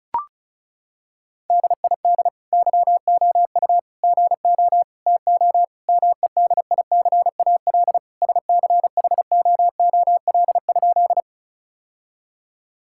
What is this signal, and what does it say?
Morse code, heard as a keyed tone, about 36 wpm, 700 Hz. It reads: DID YOU GO TO MEDICAL SCHOOL?
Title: Sentences from Top 1000 Words - Review Part 5 - 35wpm